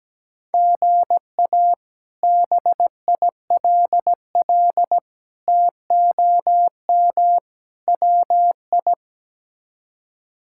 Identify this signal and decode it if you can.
Morse code, heard as a keyed tone, about 17 wpm, 700 Hz. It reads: GA BILL TOM WI
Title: K1USN SST QSO Practice - 17wpm